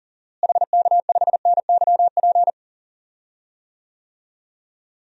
Morse code, heard as a keyed tone, about 40 wpm, 700 Hz. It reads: HK5NYP
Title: DXCC-100 Call Signs - 6 Characters - Repeated 2x - 40wpm